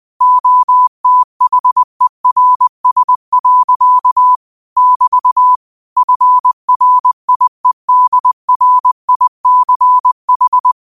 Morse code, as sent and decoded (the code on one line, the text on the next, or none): --- - .... .-. ... .-.-.- / -...- / ..-. .-. .. -.. .-. .. -.-. ....
OTHERS. <BT> FRIEDRICH